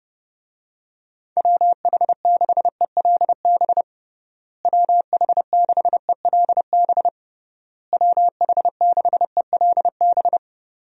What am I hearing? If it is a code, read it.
Morse code, heard as a keyed tone, about 30 wpm, 700 Hz. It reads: WH6ELB WH6ELB WH6ELB
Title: USA Call Signs - 6 Characters - Repeated 3x - 30wpm